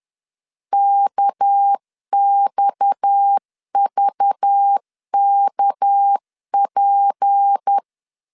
Morse code, heard as a keyed tone, about 10 wpm, 800 Hz. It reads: KXVKP
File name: CW / DET / 1 6 1500C S C 10wpm 4252